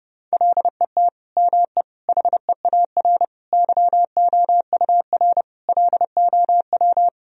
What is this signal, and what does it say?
Morse code, heard as a keyed tone, about 32 wpm, 700 Hz. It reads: LET ME HEAR YOUR LOW
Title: Sentences from Top 1000 Words - Review Part 1 - 30wpm